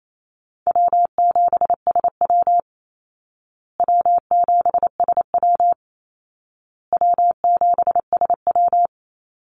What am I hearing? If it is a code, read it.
Morse code, heard as a keyed tone, about 28 wpm, 700 Hz. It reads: W7SW W7SW W7SW
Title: Call Signs Repeated 2x - 28wpm